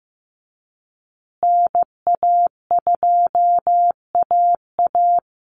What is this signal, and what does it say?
Morse code, heard as a keyed tone, about 15 wpm, 700 Hz. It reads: NA2AA
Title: Call Signs Repeated 2x - 15wpm